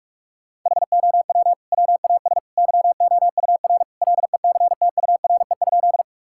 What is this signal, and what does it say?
Morse code, heard as a keyed tone, about 41 wpm, 700 Hz. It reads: HOW WAS YOUR LECTURE?
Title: Sentences from Top 1600 Words 45wpm